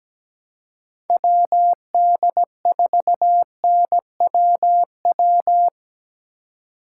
Morse code, heard as a keyed tone, about 17 wpm, 700 Hz. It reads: WD4NWW
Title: USA Call Signs - 6 Characters - 4x Character Spacing - 17wpm